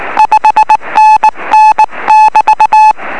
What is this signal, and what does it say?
Morse code, heard as a keyed tone, about 19 wpm, 860 Hz. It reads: 5NN<BT>
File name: 0343z dl4dtu 9cm